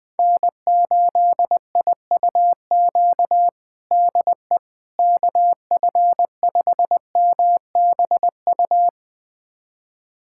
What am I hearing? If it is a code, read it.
Morse code, heard as a keyed tone, about 20 wpm, 700 Hz. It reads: N8IUQ DE KF5MBU
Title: USA Callsign DE Callsign 20wpm